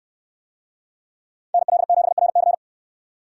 Morse code, heard as a keyed tone, about 64 wpm, 700 Hz. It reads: NL7RC